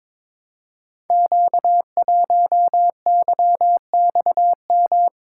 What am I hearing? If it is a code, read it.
Morse code, heard as a keyed tone, about 22 wpm, 700 Hz. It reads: Q1YXM
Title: Code Groups - 5 Letter:Numbers - 22wpm